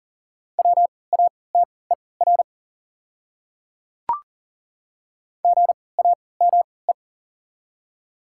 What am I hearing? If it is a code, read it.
Morse code, heard as a keyed tone, about 40 wpm, spaced as for 23 wpm, 690 Hz. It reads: WATER GAME
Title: 3x Character Spacing - Top 500 Words - 40wpm